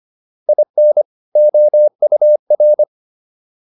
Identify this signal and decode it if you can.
Morse code, heard as a keyed tone, about 25 wpm, 600 Hz. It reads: IN OUR
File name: WD104 25